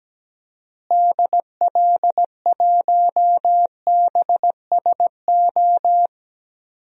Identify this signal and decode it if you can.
Morse code, heard as a keyed tone, about 17 wpm, 700 Hz. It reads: DL1BSO
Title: DXCC-100 Call Signs - 6 Characters - Repeated 2x - 17wpm